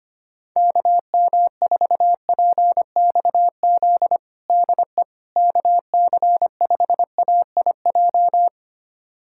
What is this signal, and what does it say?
Morse code, heard as a keyed tone, about 25 wpm, 700 Hz. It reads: KM4PXZ DE KC5AIJ
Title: USA Callsign DE Callsign - 6 Characters - 25wpm